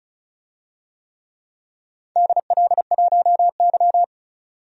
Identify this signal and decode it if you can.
Morse code, heard as a keyed tone, about 36 wpm, 700 Hz. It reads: DL1Y
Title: DXCC-100 Call Signs - 4 Characters - 35wpm